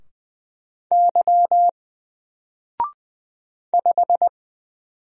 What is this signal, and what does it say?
Morse code, heard as a keyed tone, about 20 wpm, 700 Hz. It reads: Y 5
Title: Single Character (Letter:Number:Punctuation:Prosign) 20wpm